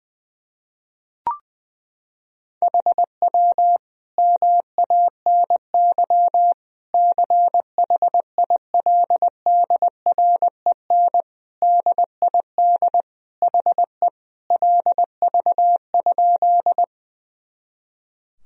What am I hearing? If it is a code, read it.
Morse code, heard as a keyed tone, about 20 wpm, 700 Hz. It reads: HW MANY CHILDREN DID HE LV?
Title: Sentences from Top 400 Words and Abbreviations 20wpm